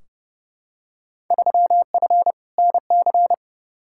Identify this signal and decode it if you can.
Morse code, heard as a keyed tone, about 29 wpm, 700 Hz. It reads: 3F NC
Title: Field Day Exchanges - 30wpm